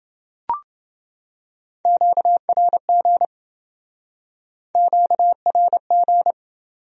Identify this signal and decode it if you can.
Morse code, heard as a keyed tone, about 27 wpm, 700 Hz. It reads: QRG QRG